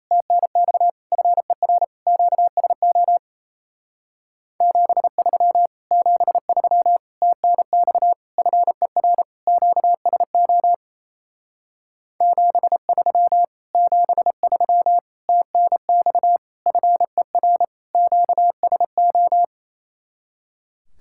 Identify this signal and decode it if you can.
Morse code, heard as a keyed tone, about 32 wpm, 700 Hz. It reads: TNX FER QSO 73 73 TNX FER QSO 73 73 TNX FER QSO
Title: Speed-Racing Common QSO Phrases Encore - 28wpm